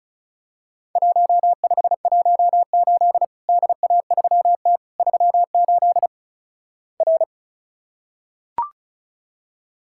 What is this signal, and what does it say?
Morse code, heard as a keyed tone, about 34 wpm, 700 Hz. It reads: 1518 DA3T 38 R
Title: Worked All Europe DX Contest - QTC Exchange - 35wpm